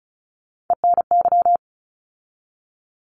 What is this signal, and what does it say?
Morse code, heard as a keyed tone, about 36 wpm, 700 Hz. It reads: ENY
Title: ARRL Sections 35wpm